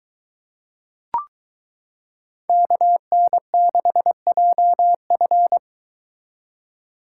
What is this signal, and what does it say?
Morse code, heard as a keyed tone, about 23 wpm, 700 Hz. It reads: KN6JF